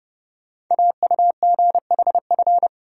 Morse code, heard as a keyed tone, about 30 wpm, 700 Hz. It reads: AUGHF
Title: Code Groups - 5 Letters - 30wpm